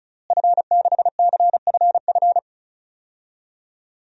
Morse code, heard as a keyed tone, about 36 wpm, 700 Hz. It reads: F6CFF
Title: DXCC-100 Call Signs - 5 Characters - Repeated 3x - 35wpm